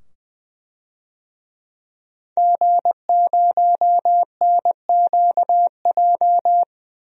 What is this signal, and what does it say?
Morse code, heard as a keyed tone, about 20 wpm, 700 Hz. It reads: G0NQJ
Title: DXCC-100 Call Signs - 5 Characters - 20wpm